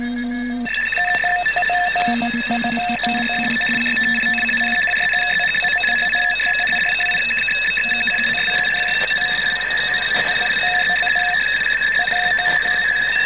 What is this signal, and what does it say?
Two-tone teleprinter signal, about 850 Hz shift.